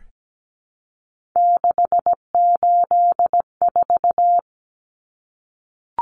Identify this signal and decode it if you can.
Morse code, heard as a keyed tone, about 17 wpm, 700 Hz. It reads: 684
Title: Numbers - 3-Digit 17wpm